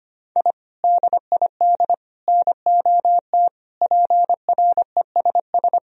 Morse code, heard as a keyed tone, about 25 wpm, 700 Hz. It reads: I DID NOT PRESS
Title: Sentences from Top 1000 Words - Review Part 1 - 25wpm